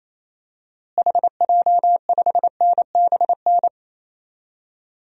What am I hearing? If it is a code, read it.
Morse code, heard as a keyed tone, about 28 wpm, 700 Hz. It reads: HJ5NBN
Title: DXCC-100 Call Signs - 6 Characters - Repeated 2x - 28wpm